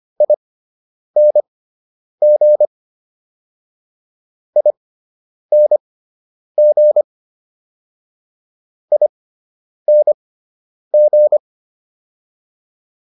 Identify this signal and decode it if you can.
Morse code, heard as a keyed tone, about 25 wpm, spaced as for 9 wpm, 600 Hz. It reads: ING ING ING